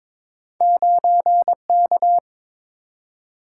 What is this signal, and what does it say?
Morse code, heard as a keyed tone, about 22 wpm, 700 Hz. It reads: 9K
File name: practice / 22wpm / Two Letter-Number Combinations - 22wpm